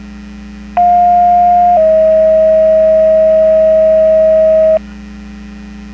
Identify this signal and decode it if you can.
Two-tone page, heard as a one-second tone, then a three-second tone, about 710 then 640 Hz. It is heard from 0.8 to 4.8 s.
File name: Portage 154.385 Paging